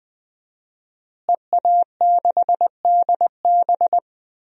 Morse code, heard as a keyed tone, about 20 wpm, 700 Hz. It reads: EA6DB
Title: DXCC-100 Call Signs - 5 Characters - Repeated 3x - 20wpm